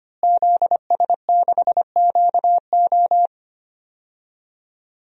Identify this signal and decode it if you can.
Morse code, heard as a keyed tone, about 25 wpm, 700 Hz. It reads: ZS6QO
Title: DXCC-100 Call Signs - 5 Characters - Repeated 3x - 25wpm